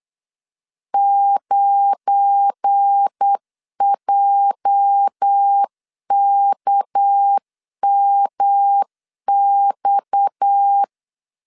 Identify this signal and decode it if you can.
Morse code, heard as a keyed tone, about 8 wpm, 800 Hz. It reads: 9JKMX
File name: CW / DET / 1 9 1500C S A 8wpm 5637